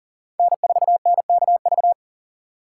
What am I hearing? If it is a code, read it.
Morse code, heard as a keyed tone, about 40 wpm, 700 Hz. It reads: N4NKV
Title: USA Call Signs - 5 Characters - 4x Character Spacing - 40wpm